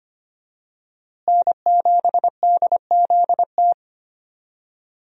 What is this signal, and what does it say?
Morse code, heard as a keyed tone, about 25 wpm, 700 Hz. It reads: N7DZT
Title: USA Call Signs - 5 Characters - Repeated 2x - 25wpm